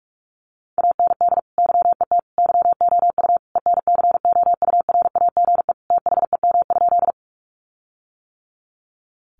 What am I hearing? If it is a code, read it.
Morse code, heard as a keyed tone, about 44 wpm, 700 Hz. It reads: AND YET YOU ENCOURAGE THEM?
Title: Sentences from Top 1400 Words 45wpm